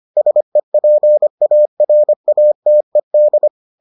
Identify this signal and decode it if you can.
Morse code, heard as a keyed tone, about 25 wpm, 600 Hz. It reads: SEPARATED